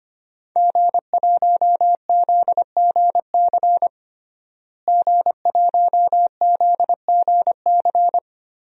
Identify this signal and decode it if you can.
Morse code, heard as a keyed tone, about 25 wpm, 700 Hz. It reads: G1ZGC G1ZGC
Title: DXCC-100 Call Signs - 5 Characters - Repeated 3x - 25wpm